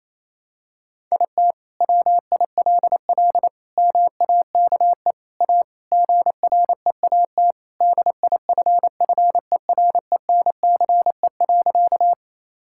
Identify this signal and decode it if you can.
Morse code, heard as a keyed tone, about 28 wpm, 700 Hz. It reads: IT WILL MAKE A GREAT DIFFERENCE.
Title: Sentences from Top 1000 Words - Review Part 5 - 28wpm